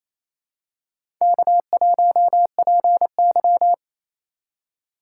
Morse code, heard as a keyed tone, about 28 wpm, 700 Hz. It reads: K1PY
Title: USA Call Signs - 4 Characters - 8x Character Spacing - 28wpm